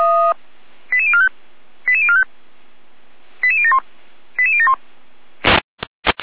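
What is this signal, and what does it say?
Five-tone selective call (ZVEI), heard as quick runs of five tones, tones of about 70 ms. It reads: 88045 88045 88071 88071
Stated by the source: Five-tone ZVEI selective call received by SDR, from audio.f85360.feuerwehr-orlamuende.org fire brigade server.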